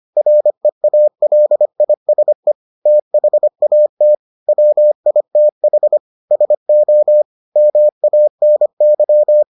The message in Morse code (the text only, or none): REALISE THAT WITH SO MANY